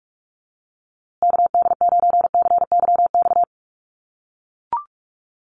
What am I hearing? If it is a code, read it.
Morse code, heard as a keyed tone, about 44 wpm, 700 Hz. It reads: KD9CYX
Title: USA Call Signs Repeated 3x 45wpm